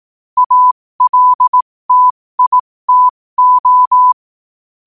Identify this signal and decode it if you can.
Morse code, heard as a keyed tone, about 18 wpm, spaced as for 15 wpm, 1000 Hz. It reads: ALTITO
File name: qotdfiles / 15wpm